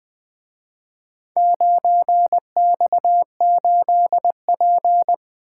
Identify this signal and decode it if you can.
Morse code, heard as a keyed tone, about 25 wpm, 700 Hz. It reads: 9X8P
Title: Speed-Racing - Four Letter-Number Combinations - 20wpm